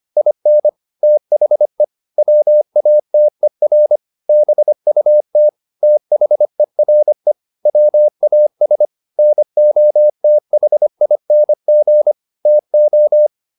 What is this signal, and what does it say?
Morse code, heard as a keyed tone, about 25 wpm, 600 Hz. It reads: IN THE WATER BUT THERE WAS NOTHING TO
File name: the old man and the sea part 1 25wpm